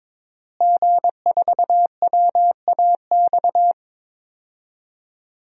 Morse code, heard as a keyed tone, about 21 wpm, 700 Hz. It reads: G4WAX
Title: DXCC-100 Call Signs - 5 Characters - Repeated 3x - 22wpm